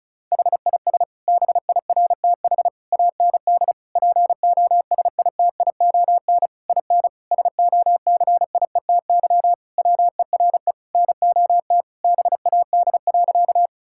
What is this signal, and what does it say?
Morse code, heard as a keyed tone, about 36 wpm, 700 Hz. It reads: HIS BIRTH AND POSITION IN SOCIETY WERE NOT BAD.
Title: Sentences from Top 1000 Words - Review Part 5 - 35wpm